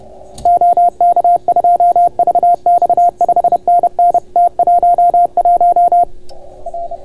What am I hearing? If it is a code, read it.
Morse code, heard as a keyed tone, about 30 wpm, 690 Hz. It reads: OK2VX 5NN T11 R